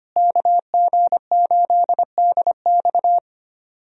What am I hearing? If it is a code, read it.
Morse code, heard as a keyed tone, about 25 wpm, 700 Hz. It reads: KG8DX
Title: USA Call Signs - 5 Characters - Repeated 2x - 25wpm